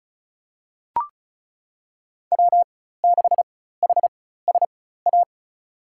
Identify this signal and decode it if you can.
Morse code, heard as a keyed tone, about 36 wpm, spaced as for 16 wpm, 700 Hz. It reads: W6HSA
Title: USA Call Signs - 5 Characters - 4x Character Spacing - 35wpm